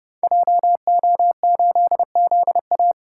Morse code, heard as a keyed tone, about 30 wpm, 700 Hz. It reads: JO8ZA
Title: Code Groups - 5 Letter:Numbers - 30wpm